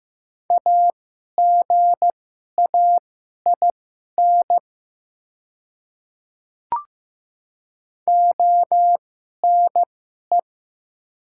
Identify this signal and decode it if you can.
Morse code, heard as a keyed tone, about 15 wpm, spaced as for 11 wpm, 700 Hz. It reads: AGAIN ONE
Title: 2x Character Spacing - Top 300 Words - 15wpm